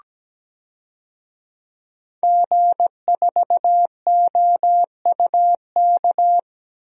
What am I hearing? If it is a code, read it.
Morse code, heard as a keyed tone, about 17 wpm, 700 Hz. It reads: G4OUK
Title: DXCC-100 Call Signs - 5 Characters - Repeated 3x - 17wpm